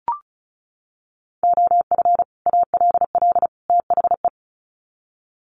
Morse code, heard as a keyed tone, about 34 wpm, 700 Hz. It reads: OF ALL THE
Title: Sets of 3 Words 35wpm